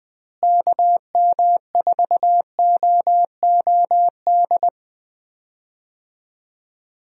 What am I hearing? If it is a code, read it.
Morse code, heard as a keyed tone, about 20 wpm, 700 Hz. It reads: KM4OOD
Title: USA Call Signs - 6 Characters - Repeated 3x - 20wpm